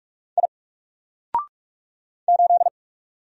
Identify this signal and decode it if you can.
Morse code, heard as a keyed tone, about 47 wpm, 700 Hz. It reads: I 8
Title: Single Character (Letter:Number:Punctuation:Prosign) 45wpm